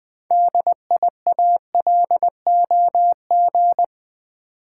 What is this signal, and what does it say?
Morse code, heard as a keyed tone, about 20 wpm, 700 Hz. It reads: DIALOG
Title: Build A Word - 2 Letters At A Time - 8-Letter Words - 20wpm